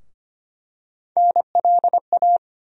Morse code, heard as a keyed tone, about 25 wpm, 700 Hz. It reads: NLA
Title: USA Call Sign Postfixes - 3 Letters 25wpm